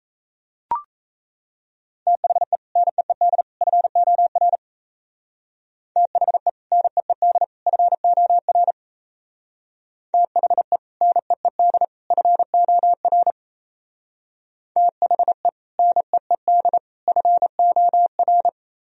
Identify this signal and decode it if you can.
Morse code, heard as a keyed tone, about 35 wpm, 700 Hz. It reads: THE NEED FOR THE NEED FOR THE NEED FOR THE NEED FOR
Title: Speed-Racing - Sets of 3 Words 28wpm